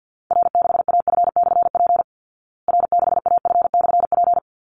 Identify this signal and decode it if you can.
Morse code, heard as a keyed tone, about 49 wpm, 700 Hz. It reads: R6AF/P R6AF/P
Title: Call Signs Repeated 3x - 50wpm